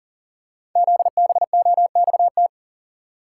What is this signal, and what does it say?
Morse code, heard as a keyed tone, about 39 wpm, 700 Hz. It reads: ZBOXT
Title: Code Groups - 5 Letters - 40wpm